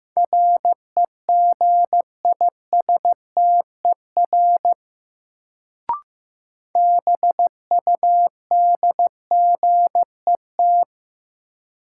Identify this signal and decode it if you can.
Morse code, heard as a keyed tone, about 15 wpm, 700 Hz. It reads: REGISTER BUDGET
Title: Sentences from Top 1200 Words - New Word Drill - 15wpm